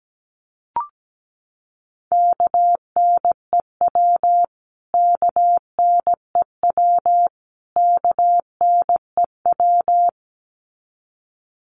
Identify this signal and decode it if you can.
Morse code, heard as a keyed tone, about 17 wpm, 700 Hz. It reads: KNEW KNEW KNEW
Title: Top 500 Words Repeated 17wpm